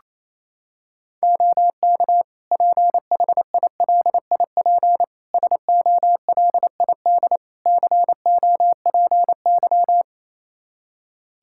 Morse code, heard as a keyed tone, about 28 wpm, 700 Hz. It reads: OK PHILIP SOLID COPY
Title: K1USN SST QSO Practice 28wpm